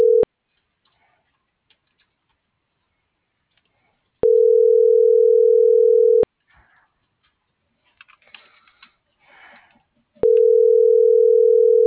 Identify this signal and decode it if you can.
Ringback tone (North America), 440 + 480 Hz.
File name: CENTRAL MULTIMODAL / out josimar.cruz - .67175